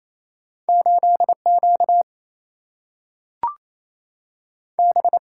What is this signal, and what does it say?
Morse code, heard as a keyed tone, about 28 wpm, 700 Hz. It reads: 8 Q B
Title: Single-Letter Single-Number 28wpm